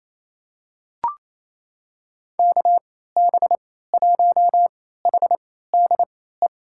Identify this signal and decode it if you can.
Morse code, heard as a keyed tone, about 28 wpm, spaced as for 16 wpm, 700 Hz. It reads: KB1HDE
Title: USA Call Signs - 6 Characters - 3x Character Spacing - 28wpm